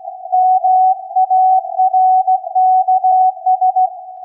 Morse code, heard as a keyed tone, about 16 wpm, 740 Hz. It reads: MARKS
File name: iz7slz / VLF / SAQ 16-11-2022